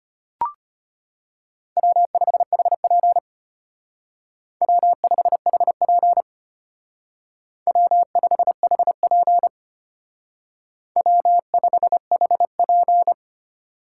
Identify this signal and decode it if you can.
Morse code, heard as a keyed tone, about 32 wpm, 700 Hz. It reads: W5HP W5HP W5HP W5HP